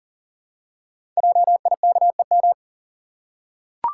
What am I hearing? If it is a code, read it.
Morse code, heard as a keyed tone, about 39 wpm, 690 Hz. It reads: JIKEM
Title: Code Groups - 5 Letters - 40wpm